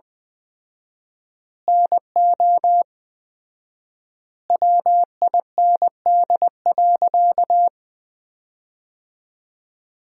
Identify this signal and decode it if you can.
Morse code, heard as a keyed tone, about 20 wpm, 700 Hz. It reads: NO WIND.
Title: 4x Word-Spacing - Sentences from Top 400 Words - 20wpm